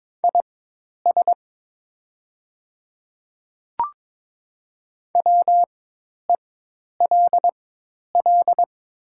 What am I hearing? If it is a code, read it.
Morse code, heard as a keyed tone, about 22 wpm, spaced as for 10 wpm, 700 Hz. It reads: IS WELL
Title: 4x Character Spacing - Top 200 Words - 22wpm